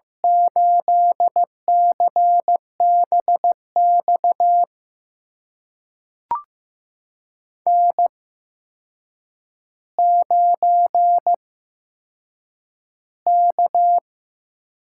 Morse code, heard as a keyed tone, about 15 wpm, spaced as for 7 wpm, 700 Hz. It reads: 8CBX N9K
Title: USA Call Signs - 8x Character Spacing - 15wpm